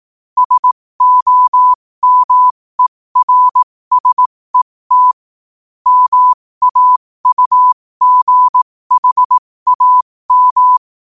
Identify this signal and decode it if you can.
Morse code, heard as a keyed tone, about 18 wpm, spaced as for 15 wpm, 1000 Hz. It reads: SOMERSET MAUGHAM
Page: 15 WPM morse code quotes